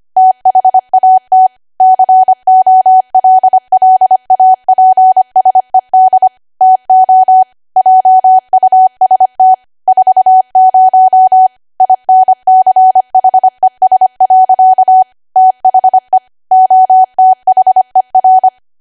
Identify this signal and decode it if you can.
Morse code, heard as a keyed tone, about 25 wpm, 750 Hz. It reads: THAT COLLAPSED TO JUST 40 INCHES. THE OTHER
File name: Archive / 25 WPM / 25WPM